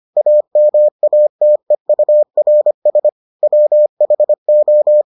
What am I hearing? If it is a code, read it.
Morse code, heard as a keyed tone, about 25 wpm, 600 Hz. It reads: AMATEURS WHO